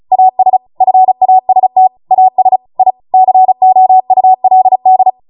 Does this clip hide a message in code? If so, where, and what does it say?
Yes: Morse code from 0.1 to 5.1 s, heard as a keyed tone, about 36 wpm, 750 Hz. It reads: AS FAST AS I COULD